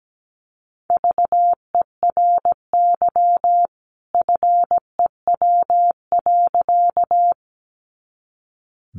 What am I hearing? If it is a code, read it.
Morse code, heard as a keyed tone, about 17 wpm, 700 Hz. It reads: VERY FEW.
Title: A Sound Of Thunder 17wpm